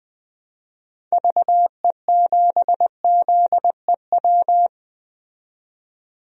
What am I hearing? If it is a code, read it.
Morse code, heard as a keyed tone, about 20 wpm, 700 Hz. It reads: VE7ZEW
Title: DXCC-100 Call Signs - 6 Characters - 20wpm